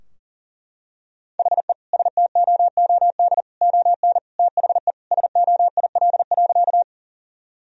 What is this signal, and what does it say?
Morse code, heard as a keyed tone, about 37 wpm, 700 Hz. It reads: HE STOOD ON THE SOIL.
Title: Sentences from Top 1000 Words - Review Part 4 - 40wpm